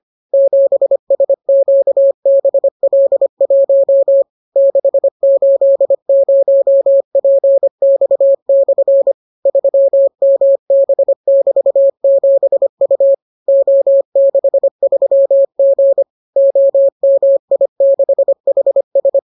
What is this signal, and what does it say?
Morse code, heard as a keyed tone, about 25 wpm, 550 Hz. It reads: 7SQBL1 680PX/ 3MB<BT>7U O63G OMI6HS